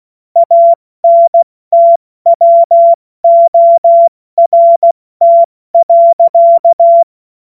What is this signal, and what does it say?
Morse code, heard as a keyed tone, about 16 wpm, 680 Hz. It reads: ANTWORT.
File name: Witz-02-September-2025-70-F80